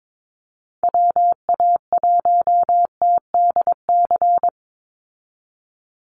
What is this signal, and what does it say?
Morse code, heard as a keyed tone, about 22 wpm, 700 Hz. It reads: WA1TDC
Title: USA Call Signs 22wpm